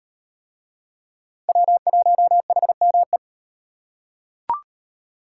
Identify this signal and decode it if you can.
Morse code, heard as a keyed tone, about 36 wpm, 700 Hz. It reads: W1HME